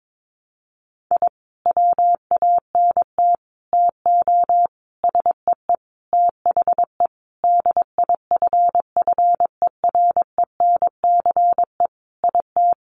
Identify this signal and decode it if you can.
Morse code, heard as a keyed tone, about 22 wpm, 700 Hz. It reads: I WANT TO SEE THE DIFFERENCE IT
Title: Sentences from Top 1300 Words 22wpm